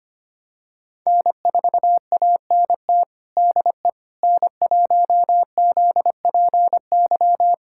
Morse code, heard as a keyed tone, about 25 wpm, 700 Hz. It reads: N4ANT DE N1ZPY
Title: USA Callsign DE Callsign - 5 Characters - 25wpm